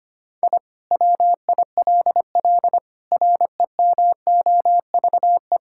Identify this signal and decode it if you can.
Morse code, heard as a keyed tone, about 25 wpm, 700 Hz. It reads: I WILL REMOVE
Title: Sentences from Top 1000 Words - Review Part 5 - 25wpm